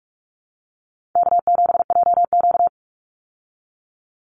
Morse code, heard as a keyed tone, about 44 wpm, 700 Hz. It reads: K7JQ
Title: Call Signs Repeated 3x - 45wpm